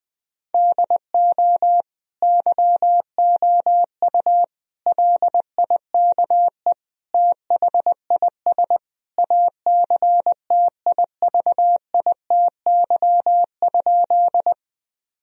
Morse code, heard as a keyed tone, about 20 wpm, 700 Hz. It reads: DO YOU LIKE THIS ACTIVITY?
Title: Sentences from Top 1000 Words - Review Part 4 - 20wpm